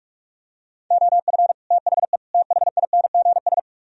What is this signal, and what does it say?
Morse code, heard as a keyed tone, about 46 wpm, 700 Hz. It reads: OF THE THINGS